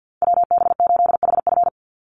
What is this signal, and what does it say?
Morse code, heard as a keyed tone, about 50 wpm, 700 Hz. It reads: WB8HF
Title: Call Signs Repeated 3x - 50wpm